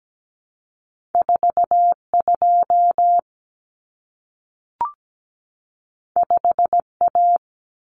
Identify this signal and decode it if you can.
Morse code, heard as a keyed tone, about 17 wpm, 700 Hz. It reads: 4 2 5 A
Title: Common Cut Numbers - 2-Digit 17wpm